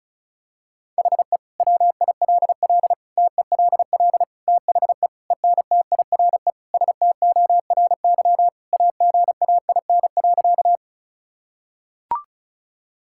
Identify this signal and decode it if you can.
Morse code, heard as a keyed tone, about 36 wpm, 700 Hz. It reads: HE WILL TELL THE ENTIRE STORY AGAIN.
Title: Sentences from Top 1000 Words - Review Part 5 - 35wpm